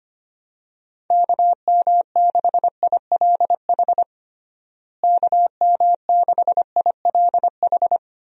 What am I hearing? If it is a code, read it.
Morse code, heard as a keyed tone, about 25 wpm, 700 Hz. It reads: KM6ILH KM6ILH
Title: USA Call Signs - 6 Characters - Repeated 2x - 25wpm